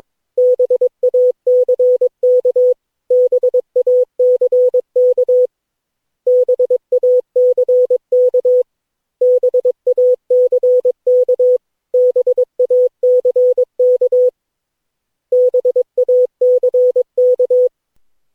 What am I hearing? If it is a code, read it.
Morse code, heard as a keyed tone, about 22 wpm, 500 Hz. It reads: BACK BACK BACK BACK BACK BACK
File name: cw / Session9 cw training